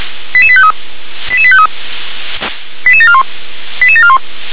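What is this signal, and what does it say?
Five-tone selective call (ZVEI), heard as quick runs of five tones, tones of about 70 ms. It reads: 88853 88853 88741 88741
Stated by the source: Decoded by SDR as ZVEI five-tone selective call.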